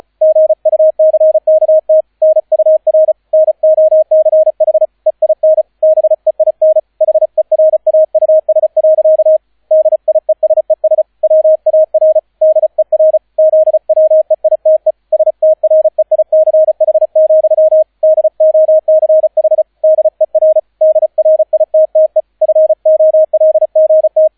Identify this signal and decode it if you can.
Morse code, heard as a keyed tone, about 36 wpm, 630 Hz, with radio noise. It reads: GUCKT NUR NOCH EIN BEIN HERAUS. DIESES WAR DER ZWEITE STREICH, DOCH DER DRITTE FOLGT